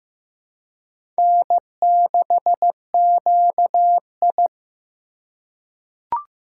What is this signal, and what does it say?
Morse code, heard as a keyed tone, about 15 wpm, 700 Hz. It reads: N6QI
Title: USA Call Signs - 4 Characters - 4x Character Spacing - 15wpm